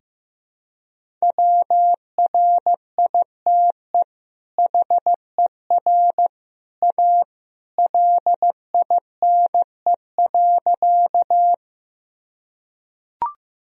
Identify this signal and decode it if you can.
Morse code, heard as a keyed tone, about 15 wpm, 700 Hz. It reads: WRITE HER A LINE.
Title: Sentences from Top 1000 Words - Review Part 1 - 15wpm